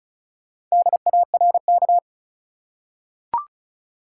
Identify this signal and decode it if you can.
Morse code, heard as a keyed tone, about 35 wpm, 700 Hz. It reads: DARK
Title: Four Letter Words - 35wpm